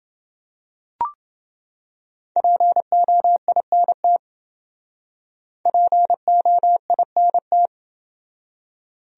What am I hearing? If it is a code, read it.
Morse code, heard as a keyed tone, about 27 wpm, 700 Hz. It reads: POINT POINT